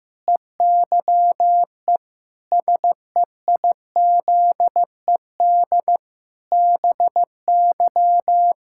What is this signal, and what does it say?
Morse code, heard as a keyed tone, about 15 wpm, 700 Hz. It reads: EYE SEIZED BY
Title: Pangram Challenge v2 - 15wpm